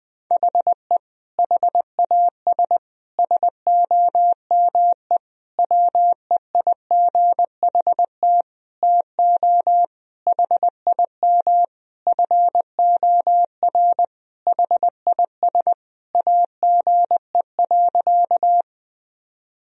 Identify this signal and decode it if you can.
Morse code, heard as a keyed tone, about 21 wpm, 700 Hz. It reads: HE HAS SOME WEIGHT TO HIM FOR HIS AGE.
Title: Sentences from Top 1000 Words - Review Part 3 - 20wpm